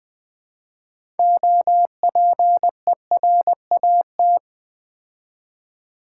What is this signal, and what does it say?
Morse code, heard as a keyed tone, about 20 wpm, 700 Hz. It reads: OPERAT
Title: Build A Word - 3 Letters At A Time - 9-Letter Words - 20wpm